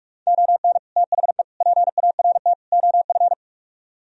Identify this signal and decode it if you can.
Morse code, heard as a keyed tone, about 55 wpm, 700 Hz. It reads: ON THE PART OF
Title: Speed-Racing - Sets of 4 Words 45wpm